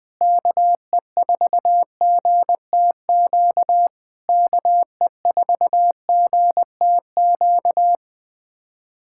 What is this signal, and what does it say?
Morse code, heard as a keyed tone, about 20 wpm, 700 Hz. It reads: KE4GTQ KE4GTQ
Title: POTA QSO Practice - 20wpm